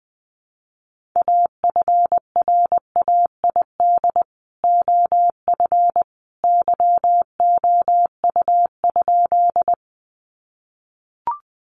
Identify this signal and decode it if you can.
Morse code, heard as a keyed tone, about 20 wpm, 700 Hz. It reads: AFRAID OF YOU?
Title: Sentences from Top 1500 Words 20wpm